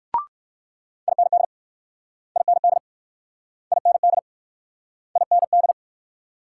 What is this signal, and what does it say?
Morse code, heard as a keyed tone, about 64 wpm, 700 Hz. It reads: IND IND IND IND